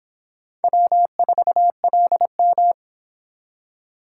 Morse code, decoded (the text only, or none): W4LM